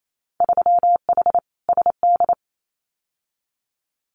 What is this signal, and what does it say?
Morse code, heard as a keyed tone, about 28 wpm, 700 Hz. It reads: 3H SD
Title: Winter Field Day Exchanges Encore 28wpm